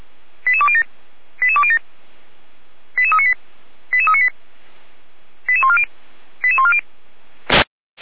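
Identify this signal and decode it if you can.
Five-tone selective call (ZVEI), heard as quick runs of five tones, tones of about 70 ms. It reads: 88207 88207 88398 88398 88140 88140
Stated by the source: ZVEI five-tone call received by SDR.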